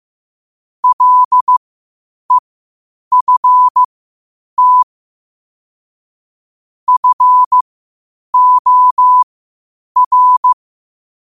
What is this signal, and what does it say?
Morse code, heard as a keyed tone, about 15 wpm, spaced as for 8 wpm, 1000 Hz. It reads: LEFT FOR
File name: qotdfiles / 8wpm